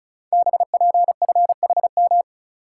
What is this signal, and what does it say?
Morse code, heard as a keyed tone, about 36 wpm, 700 Hz. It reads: BPFHM
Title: Code Groups - 5 Letters - 35wpm